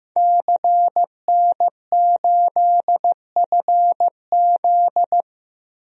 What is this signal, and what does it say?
Morse code, heard as a keyed tone, about 15 wpm, 700 Hz. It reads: CN8FZ
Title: DXCC-100 Call Signs - 5 Characters - 15wpm